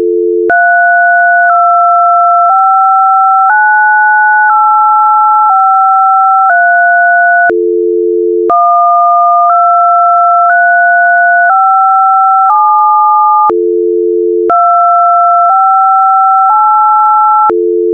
DTMF, heard as telephone touch tones, tones of about 1010 ms. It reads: 3259853 12357 258